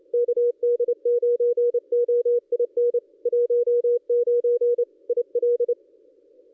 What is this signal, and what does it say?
Morse code, heard as a keyed tone, about 29 wpm, 440 Hz. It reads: KD9OIN 19 IL
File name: contests / ss-cw-k5tr-op-n5rz -1400